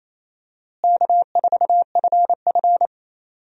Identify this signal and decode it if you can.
Morse code, heard as a keyed tone, about 27 wpm, 700 Hz. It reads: K4FF
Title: USA Call Signs - 4 Characters - Repeated 3x - 28wpm